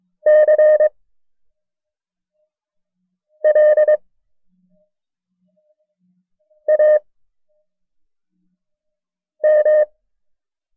Morse code, heard as a keyed tone, about 23 wpm, spaced as for 3 wpm, 610 Hz. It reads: CLAM
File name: FreeFiles / CW-practice / KB6MT-tape1-side2